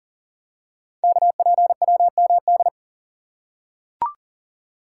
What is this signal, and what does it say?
Morse code, heard as a keyed tone, about 39 wpm, 690 Hz. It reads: KPWMD
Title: Code Groups - 5 Letters - 40wpm